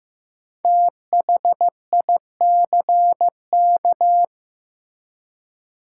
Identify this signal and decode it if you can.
Morse code, heard as a keyed tone, about 15 wpm, 700 Hz. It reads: THICK
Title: Five Letter Words - 15wpm